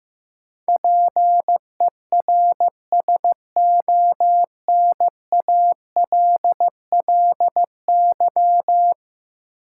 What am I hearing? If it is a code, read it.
Morse code, heard as a keyed tone, about 15 wpm, 700 Hz. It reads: PERSONALLY
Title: Sentences from Top 1400 Words - New Word Drill Alt - 15wpm